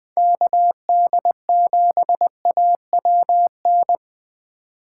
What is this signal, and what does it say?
Morse code, heard as a keyed tone, about 20 wpm, 700 Hz. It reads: KD7AWN
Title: USA Call Signs - 6 Characters - 2x Character Spacing - 20wpm